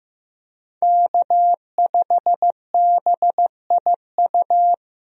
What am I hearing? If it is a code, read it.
Morse code, heard as a keyed tone, about 15 wpm, 700 Hz. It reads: K5BIU
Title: USA Call Signs - 5 Characters - Repeated 2x - 15wpm